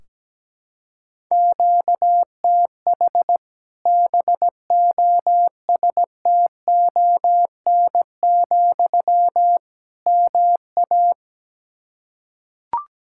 Morse code, heard as a keyed tone, about 17 wpm, 700 Hz. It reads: QTH BOSTON, MA
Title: Practice QSO - Exchange 1 - Beginning - 17wpm